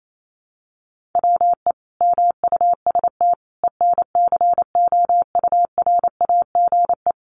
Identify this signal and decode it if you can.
Morse code, heard as a keyed tone, about 28 wpm, 700 Hz. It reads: WE MUST ENCOURAGE
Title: Sentences from Top 1400 Words 28wpm